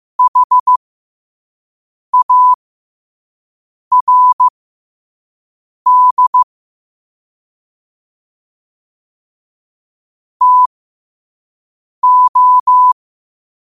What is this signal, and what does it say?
Morse code, heard as a keyed tone, about 15 wpm, spaced as for 5 wpm, 1000 Hz. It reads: HARD TO
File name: qotdfiles / 5wpm